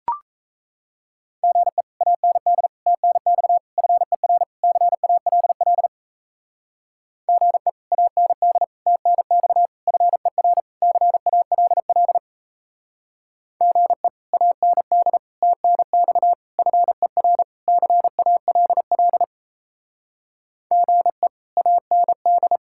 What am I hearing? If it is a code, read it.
Morse code, heard as a keyed tone, about 34 wpm, 700 Hz. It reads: GE AND TNX FER CALL GE AND TNX FER CALL GE AND TNX FER CALL GE AND
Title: Speed-Racing Common QSO Phrases Encore - 28wpm